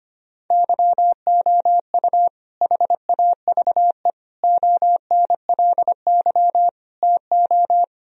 Morse code, heard as a keyed tone, about 25 wpm, 700 Hz. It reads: YOU HAVE ONLY TO
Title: Sentences from Top 1000 Words - Review Part 4 - 25wpm